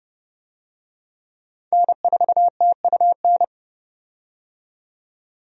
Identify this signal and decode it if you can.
Morse code, heard as a keyed tone, about 30 wpm, 700 Hz. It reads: N4TUN
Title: USA Call Signs - 5 Characters - 30wpm